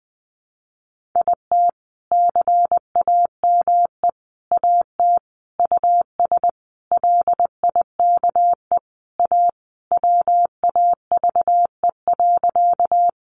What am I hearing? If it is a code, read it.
Morse code, heard as a keyed tone, about 20 wpm, 700 Hz. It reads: IT CAME AT US LIKE A WAVE.
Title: Sentences from Top 1400 Words 20wpm